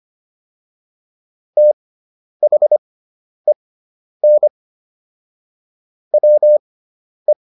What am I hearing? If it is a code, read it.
Morse code, heard as a keyed tone, about 25 wpm, spaced as for 10 wpm, 600 Hz. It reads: THEN WE